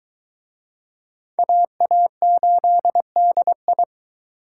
Morse code, heard as a keyed tone, about 22 wpm, 700 Hz. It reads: AA8DI